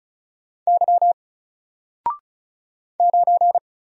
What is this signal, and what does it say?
Morse code, heard as a keyed tone, about 35 wpm, 700 Hz. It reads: Y 9
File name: Single-Letter-Number-35wpm